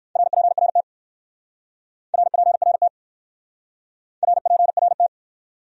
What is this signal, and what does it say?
Morse code, heard as a keyed tone, about 56 wpm, 700 Hz. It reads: RPRT RPRT RPRT
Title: Speed-Racing Abbreviations - 45wpm